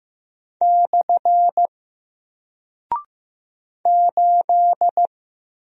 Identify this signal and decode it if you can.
Morse code, heard as a keyed tone, about 15 wpm, 700 Hz. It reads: / 8
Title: Single Character (Letter:Number:Punctuation:Prosign) Alt 15wpm